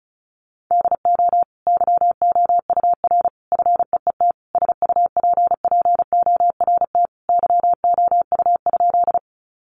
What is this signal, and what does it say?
Morse code, heard as a keyed tone, about 34 wpm, 700 Hz. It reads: DO YOUR FEET SUPPORT YOU?
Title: Sentences from Top 800 Words 35wpm